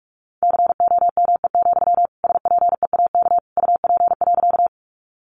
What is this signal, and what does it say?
Morse code, heard as a keyed tone, about 45 wpm, 700 Hz. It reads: COME, SPEAK UP.
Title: Sentences from Top 1100 Words 45wpm